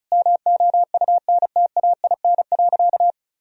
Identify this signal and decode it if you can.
Morse code, heard as a keyed tone, about 35 wpm, 700 Hz. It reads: MOUNTAIN.
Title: Sentences from Top 1000 Words - Review Part 1 - 35wpm